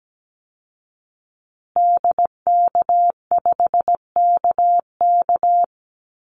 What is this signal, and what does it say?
Morse code, heard as a keyed tone, about 17 wpm, 700 Hz. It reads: DK5KK
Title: Call Signs Repeated 2x - 17wpm